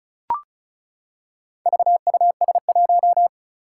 Morse code, heard as a keyed tone, about 36 wpm, 700 Hz. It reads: VUS1
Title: Four Challenging Letter-Number Combinations 35wpm